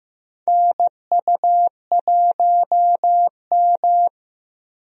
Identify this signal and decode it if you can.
Morse code, heard as a keyed tone, about 15 wpm, 700 Hz. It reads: NU1M
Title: USA Call Signs - 4 Characters - Repeated 3x - 15wpm